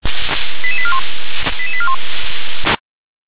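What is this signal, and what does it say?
Five-tone selective call (ZVEI), heard as quick runs of five tones, tones of about 70 ms. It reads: 88841 88841
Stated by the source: ZVEI five-tone selective call received by SDR.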